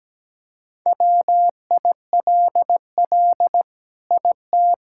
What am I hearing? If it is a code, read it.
Morse code, heard as a keyed tone, about 17 wpm, 700 Hz. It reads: WILL IT
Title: Sentences from Top 1000 Words - Review Part 6 - 17wpm